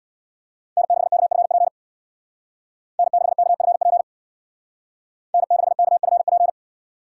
Morse code, heard as a keyed tone, about 62 wpm, 700 Hz. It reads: N6KFP N6KFP N6KFP
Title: Speed Racing - USA Call Signs 50wpm